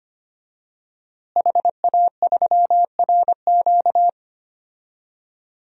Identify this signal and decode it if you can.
Morse code, heard as a keyed tone, about 25 wpm, 700 Hz. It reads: HA3RQ
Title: DXCC-100 Call Signs - 5 Characters - Repeated 2x - 25wpm